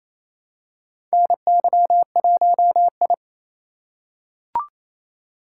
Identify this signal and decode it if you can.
Morse code, heard as a keyed tone, about 27 wpm, 700 Hz. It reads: NY1I
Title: USA Call Signs - 4 Characters - 4x Character Spacing - 28wpm